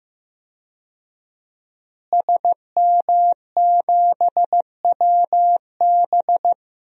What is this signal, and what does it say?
Morse code, heard as a keyed tone, about 15 wpm, 700 Hz. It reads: SM7WB
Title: DXCC-100 Call Signs - 5 Characters - Repeated 2x - 15wpm